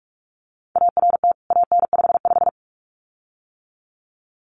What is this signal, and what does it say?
Morse code, heard as a keyed tone, about 44 wpm, 700 Hz. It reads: ART AN55
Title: CWT Exchange 45wpm